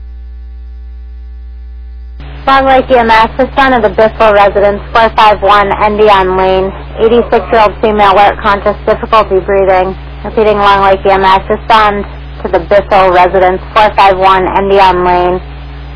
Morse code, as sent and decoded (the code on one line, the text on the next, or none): none